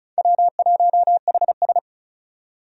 Morse code, heard as a keyed tone, about 36 wpm, 700 Hz. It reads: W1HS
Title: USA Call Signs - 4 Characters - 35wpm